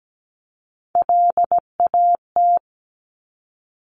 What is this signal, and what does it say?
Morse code, heard as a keyed tone, about 17 wpm, 700 Hz. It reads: LAT
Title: Top Three Letter Combinations in Words 17wpm